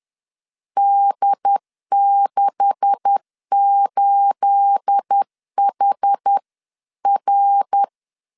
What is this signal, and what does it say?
Morse code, heard as a keyed tone, about 10 wpm, 800 Hz. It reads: D68HR